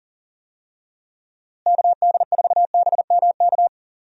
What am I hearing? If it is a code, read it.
Morse code, heard as a keyed tone, about 40 wpm, 700 Hz. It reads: KD4BMK
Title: USA Call Signs - 6 Characters - Repeated 3x - 40wpm